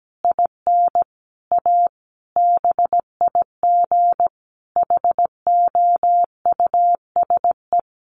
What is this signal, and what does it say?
Morse code, heard as a keyed tone, about 17 wpm, 700 Hz. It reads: IN A BIG HOUSE
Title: Additional Easy Phrases 17wpm